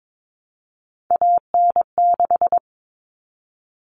Morse code, heard as a keyed tone, about 22 wpm, 700 Hz. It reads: AN6
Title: All Cut Numbers - 3-Digit 22wpm